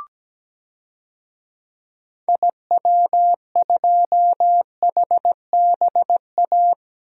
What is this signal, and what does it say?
Morse code, heard as a keyed tone, about 17 wpm, 700 Hz. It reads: IW2HBA
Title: DXCC-100 Call Signs - 6 Characters - Repeated 3x - 17wpm